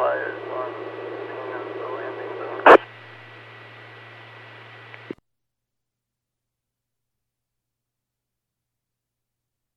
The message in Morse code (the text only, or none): none